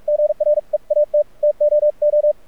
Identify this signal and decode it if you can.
Morse code, heard as a keyed tone, about 45 wpm, spaced as for 38 wpm, 600 Hz, with radio noise. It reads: GREAT TOO
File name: archived / CRSnet